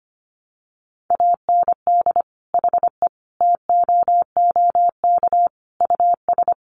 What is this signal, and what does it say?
Morse code, heard as a keyed tone, about 25 wpm, 700 Hz. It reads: AND HE TOOK US
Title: Sentences from Top 1500 Words 25wpm